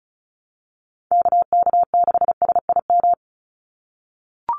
Morse code, heard as a keyed tone, about 34 wpm, 700 Hz. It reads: KK6SIM
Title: USA Call Signs 35wpm